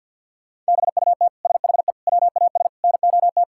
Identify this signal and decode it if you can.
Morse code, heard as a keyed tone, about 50 wpm, 700 Hz. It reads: BUT SHE WAS NOT
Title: Sentences from Top 1000 Words - Review Part 6 - 50wpm